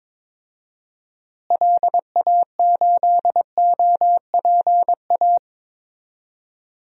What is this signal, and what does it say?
Morse code, heard as a keyed tone, about 22 wpm, 700 Hz. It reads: LA8OPA
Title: DXCC-100 Call Signs - 6 Characters - Repeated 2x - 22wpm